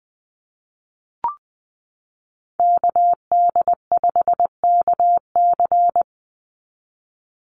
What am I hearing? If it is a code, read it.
Morse code, heard as a keyed tone, about 20 wpm, 700 Hz. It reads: KD5KC
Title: NA SOTA Activator Callsigns 20wpm